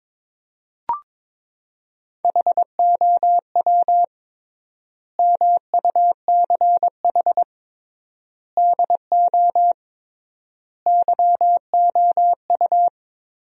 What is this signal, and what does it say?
Morse code, heard as a keyed tone, about 22 wpm, 700 Hz. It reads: HOW MUCH DO YOU
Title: 3x Word-Spacing - Sentences from Top 200 Words - 22wpm